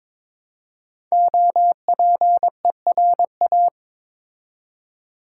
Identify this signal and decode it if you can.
Morse code, heard as a keyed tone, about 22 wpm, 700 Hz. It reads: OPERA
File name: 5 Letter Words - First 3 Letters Given - 22wpm